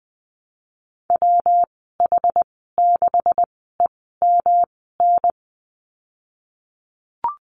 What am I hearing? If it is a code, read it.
Morse code, heard as a keyed tone, about 20 wpm, spaced as for 15 wpm, 700 Hz. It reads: WH6EMN
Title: 154 Lesson 25g - Letter p - Call Signs 20wpm